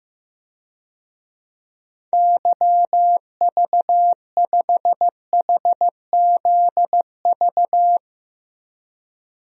Morse code, heard as a keyed tone, about 15 wpm, 700 Hz. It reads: YV5HZV
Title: DXCC-100 Call Signs - 6 Characters - 15wpm